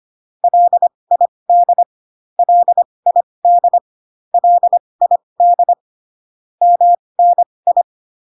Morse code, heard as a keyed tone, about 25 wpm, spaced as for 20 wpm, 700 Hz. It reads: LID LID LID MNI